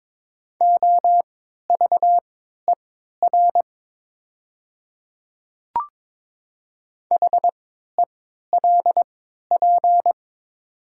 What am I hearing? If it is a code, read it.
Morse code, heard as a keyed tone, about 22 wpm, spaced as for 12 wpm, 690 Hz. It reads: OVER HELP
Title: 3x Character Spacing - Top 300 Words - 22wpm